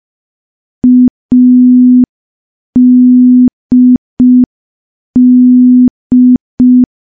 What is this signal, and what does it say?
Morse code, heard as a keyed tone, about 5 wpm, 250 Hz. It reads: ADD
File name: morse / morse-training-05WPM